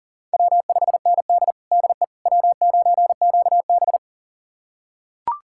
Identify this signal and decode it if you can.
Morse code, heard as a keyed tone, about 40 wpm, 700 Hz. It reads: W5ND DE W9QB
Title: USA Callsign DE Callsign - 4 Characters - 40wpm